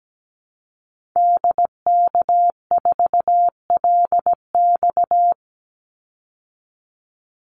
Morse code, heard as a keyed tone, about 17 wpm, 700 Hz. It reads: DK4LX
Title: Call Signs - 17wpm